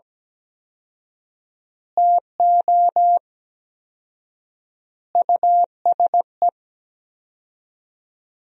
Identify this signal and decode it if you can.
Morse code, heard as a keyed tone, about 17 wpm, 700 Hz. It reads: TO USE
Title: 4x Word-Spacing - Sentences from Top 500 Words - 17wpm